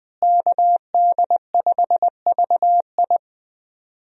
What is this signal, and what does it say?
Morse code, heard as a keyed tone, about 20 wpm, 700 Hz. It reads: KD5VI
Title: USA Call Signs - 5 Characters - Repeated 3x - 20wpm